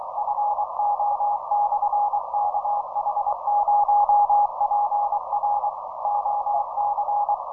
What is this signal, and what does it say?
Morse code, heard as a keyed tone, about 24 wpm, 830 Hz, with noise. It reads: FO/KH0PR OC